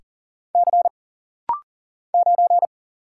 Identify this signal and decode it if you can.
Morse code, heard as a keyed tone, about 40 wpm, 700 Hz. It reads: C 9
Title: Single Letter-Number Alt - Rapid-Fire 40wpm